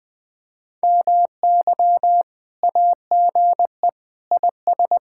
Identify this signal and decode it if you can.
Morse code, heard as a keyed tone, about 20 wpm, 700 Hz. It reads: MY AGE IS